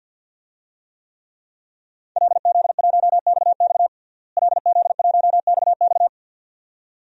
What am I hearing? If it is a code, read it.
Morse code, heard as a keyed tone, about 51 wpm, 700 Hz. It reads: LZ1XX LZ1XX
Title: DXCC-100 Call Signs - 5 Characters - Repeated 2x - 50wpm